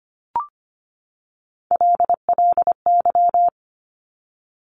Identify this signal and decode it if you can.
Morse code, heard as a keyed tone, about 25 wpm, 700 Hz. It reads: LLY
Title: Top Three Letter Combinations in Words 25wpm